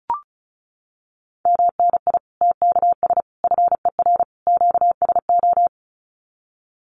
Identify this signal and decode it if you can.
Morse code, heard as a keyed tone, about 34 wpm, 700 Hz. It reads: MNI TKS FER QSO
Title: Common QSO Phrases 35wpm